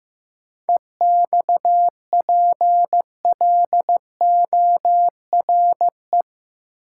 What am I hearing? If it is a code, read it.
Morse code, heard as a keyed tone, about 15 wpm, 700 Hz. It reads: EXPLORE
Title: Sentences from Top 1100 Words - New Word Drill - 15wpm